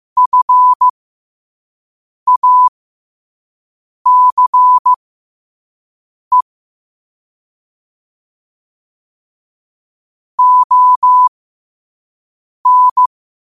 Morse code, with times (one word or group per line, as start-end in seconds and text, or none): 0.2-6.4 FACE
10.4-13.1 ON